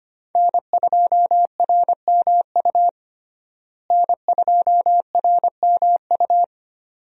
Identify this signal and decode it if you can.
Morse code, heard as a keyed tone, about 25 wpm, 700 Hz. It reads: N2RMU N2RMU
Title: USA Call Signs - 5 Characters - Repeated 2x - 25wpm